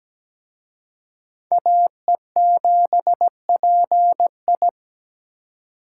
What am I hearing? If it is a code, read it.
Morse code, heard as a keyed tone, about 20 wpm, 700 Hz. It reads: AE7PI
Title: Speed Racing - USA Call Signs 17wpm